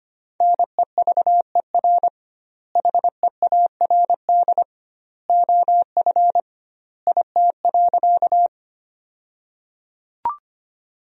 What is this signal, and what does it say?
Morse code, heard as a keyed tone, about 25 wpm, 700 Hz. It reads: NEVER HEARD OF IT.
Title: 2x Word-Spacing - Sentences from Top 400 Words - 25wpm